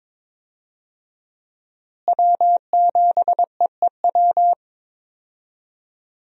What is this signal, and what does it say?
Morse code, heard as a keyed tone, about 22 wpm, 700 Hz. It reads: W7EEW
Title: USA Call Signs - 5 Characters - Repeated 2x - 22wpm